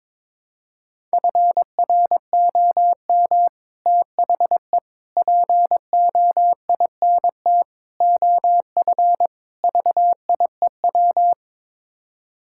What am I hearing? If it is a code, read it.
Morse code, heard as a keyed tone, about 22 wpm, 700 Hz. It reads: FROM THE POINT OF VIEW
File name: practice / 22wpm / Build A Phrase - 5-Word Phrases - 22wpm